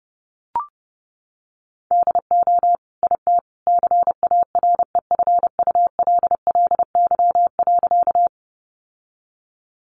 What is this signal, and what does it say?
Morse code, heard as a keyed tone, about 30 wpm, 700 Hz. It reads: DO IT CAREFULLY.
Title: Sentences from Top 1300 Words 30wpm